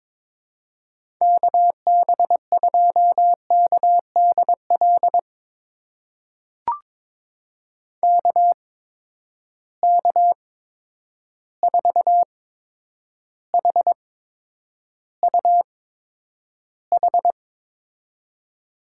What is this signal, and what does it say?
Morse code, heard as a keyed tone, about 22 wpm, spaced as for 9 wpm, 700 Hz. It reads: KB2KDL KK4HUH
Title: USA Call Signs - 6 Characters - 8x Character Spacing - 22wpm